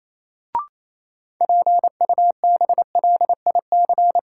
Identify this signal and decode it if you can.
Morse code, heard as a keyed tone, about 28 wpm, 700 Hz. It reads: PUBLIC
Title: Sets of 2 Words - New Word Drill - 28wpm